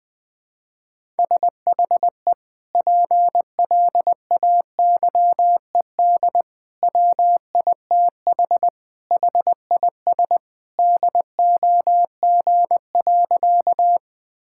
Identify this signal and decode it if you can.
Morse code, heard as a keyed tone, about 20 wpm, 700 Hz. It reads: SHE PLAYED WITH HIS DOG.
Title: Sentences with 2-Word Phrases - 20wpm